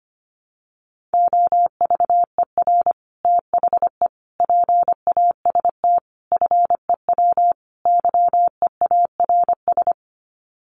Morse code, heard as a keyed tone, about 25 wpm, 700 Hz. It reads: OVER THE PAST FEW YEARS
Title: Sets of 5 Words Encore 25wpm